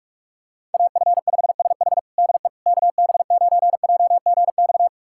Morse code, heard as a keyed tone, about 43 wpm, 700 Hz. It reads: AF5SH DE KB9JGX
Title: USA Callsign DE Callsign 45wpm